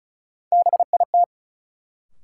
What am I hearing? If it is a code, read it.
Morse code, heard as a keyed tone, about 36 wpm, 700 Hz. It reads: BIT
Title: Sets of 3 Words - New Word Drill - 35wpm